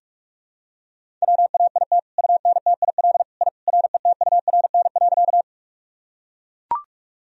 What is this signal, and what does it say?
Morse code, heard as a keyed tone, about 41 wpm, 700 Hz. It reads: WAIT UNTIL I RETURN.
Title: Sentences from Top 1000 Words - Review Part 1 - 45wpm